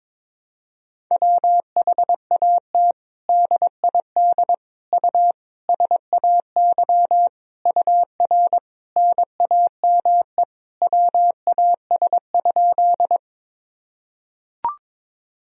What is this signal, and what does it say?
Morse code, heard as a keyed tone, about 21 wpm, 700 Hz. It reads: WHAT DID U SAY UR NAME WAS?
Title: Sentences from Top 200 Words and Abbreviations 22wpm